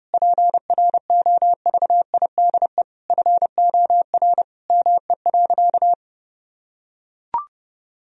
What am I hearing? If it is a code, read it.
Morse code, heard as a keyed tone, about 30 wpm, 700 Hz. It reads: PROVIDE FOR ME.
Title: Sentences from Top 1000 Words - Review Part 3 - 30wpm